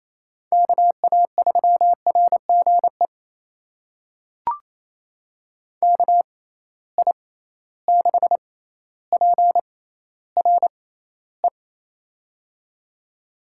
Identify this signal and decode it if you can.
Morse code, heard as a keyed tone, about 28 wpm, spaced as for 14 wpm, 700 Hz. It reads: KA3RGE KI6PRE
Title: USA Call Signs - 6 Characters - 6x Character Spacing - 28wpm